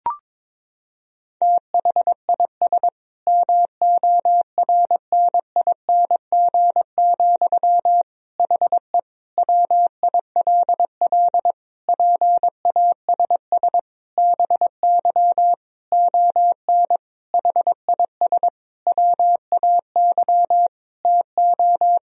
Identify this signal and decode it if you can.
Morse code, heard as a keyed tone, about 22 wpm, 700 Hz. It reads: THIS MORNING, HE WILL PASS BY ON HIS WAY TO
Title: Sentences from Top 1000 Words - Review Part 1 - 22wpm